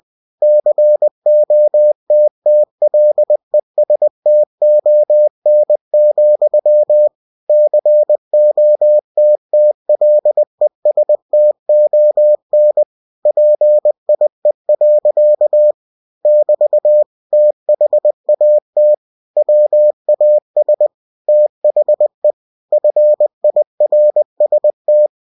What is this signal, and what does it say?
Morse code, heard as a keyed tone, about 20 wpm, 600 Hz. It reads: COTTLESTON, COTTLESTON PIE. <BT> THAT WAS THE FIRST
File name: cw-books / winnie-the-pooh / winnie-20wpm-page-0045